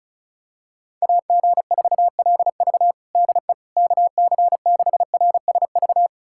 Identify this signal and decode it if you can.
Morse code, heard as a keyed tone, about 35 wpm, 700 Hz. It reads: AG4LV DE KC6RSV
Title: USA Callsign DE Callsign 35wpm